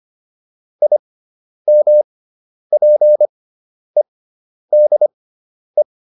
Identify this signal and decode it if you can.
Morse code, heard as a keyed tone, about 25 wpm, spaced as for 10 wpm, 600 Hz. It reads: IMPEDE